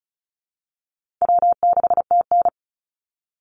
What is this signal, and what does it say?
Morse code, heard as a keyed tone, about 34 wpm, 700 Hz. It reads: W6TN
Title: Easy Call Signs Repeated 3x - 35wpm